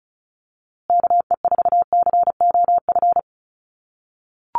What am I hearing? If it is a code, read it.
Morse code, heard as a keyed tone, about 34 wpm, 700 Hz. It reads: KE4COF
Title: USA Call Signs Repeated 3x 35wpm